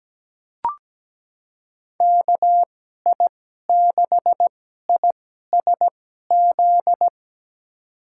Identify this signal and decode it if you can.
Morse code, heard as a keyed tone, about 17 wpm, spaced as for 12 wpm, 700 Hz. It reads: KI6ISZ
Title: USA Call Signs - 6 Characters - 2x Character Spacing - 17wpm